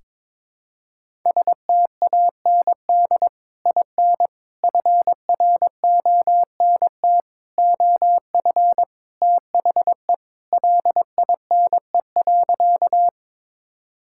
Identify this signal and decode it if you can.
Morse code, heard as a keyed tone, about 22 wpm, 700 Hz. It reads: STAND IN FRONT OF THE LINE.
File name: practice / 22wpm / Sentences with 3-Word Phrases - 22wpm